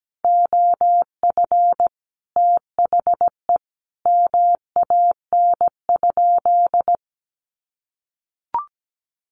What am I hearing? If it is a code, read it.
Morse code, heard as a keyed tone, about 17 wpm, 700 Hz. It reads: OF THE MAN?
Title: Sentences from Top 1300 Words 17wpm